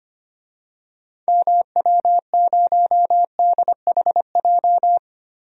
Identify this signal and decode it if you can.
Morse code, heard as a keyed tone, about 25 wpm, 700 Hz. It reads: MW0DHJ
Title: DXCC-100 Call Signs - 6 Characters - 25wpm